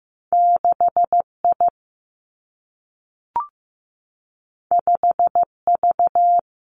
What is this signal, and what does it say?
Morse code, heard as a keyed tone, about 15 wpm, 700 Hz. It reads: 6 I 5 V
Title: Two Challenging Letter-Number Combinations 15wpm